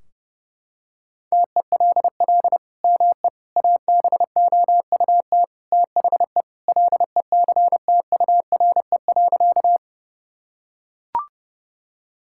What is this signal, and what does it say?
Morse code, heard as a keyed tone, about 32 wpm, 700 Hz. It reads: TELL ME ABOUT THE LECTURE.
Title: Sentences from Top 1600 Words 30wpm